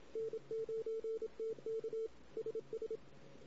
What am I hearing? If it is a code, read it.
Morse code, heard as a keyed tone, about 26 wpm, 450 Hz. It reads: N9TK SS
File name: CW / 2008-ss-cw-k5tr-op-n5rz -16:00